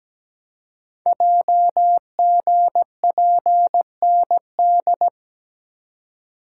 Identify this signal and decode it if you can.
Morse code, heard as a keyed tone, about 17 wpm, 700 Hz. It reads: JGPND
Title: Code Groups - 5 Letters - 17wpm